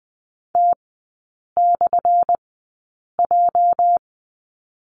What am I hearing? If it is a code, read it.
Morse code, heard as a keyed tone, about 20 wpm, spaced as for 8 wpm, 700 Hz. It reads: T/J
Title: 205 Lesson 33e - Letter j - Three Characters 20wpm